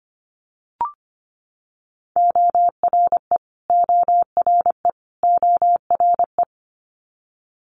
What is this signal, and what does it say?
Morse code, heard as a keyed tone, about 25 wpm, 700 Hz. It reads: ORE ORE ORE
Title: Multi-letter Phonemes Repeated 3x 25wpm